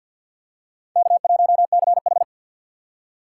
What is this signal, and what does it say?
Morse code, heard as a keyed tone, about 50 wpm, 700 Hz. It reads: K1CH
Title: USA Call Signs - 4x Character Spacing - 50wpm